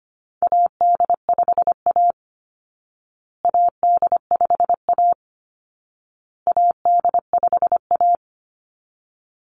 Morse code, heard as a keyed tone, about 25 wpm, 700 Hz. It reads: AD5A AD5A AD5A
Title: Call Signs Repeated 2x - 25wpm